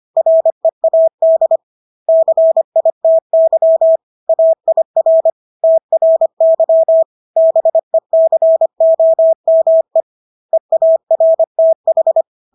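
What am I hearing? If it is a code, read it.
Morse code, heard as a keyed tone, about 25 wpm, 650 Hz. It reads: READ CITY AIR TRY BECOME EARTH
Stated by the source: Morse practice of English words at 25 wpm from CW med Gnister.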